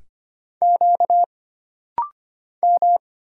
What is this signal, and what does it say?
Morse code, heard as a keyed tone, about 25 wpm, 700 Hz. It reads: Q M
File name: Single Character (Letter:Number:Punctuation:Prosign) - Rapid-Fire - 25wpm